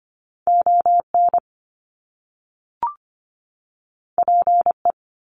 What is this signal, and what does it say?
Morse code, heard as a keyed tone, about 25 wpm, 700 Hz. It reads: ON PE
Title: Canadian Provinces and Territories 25wpm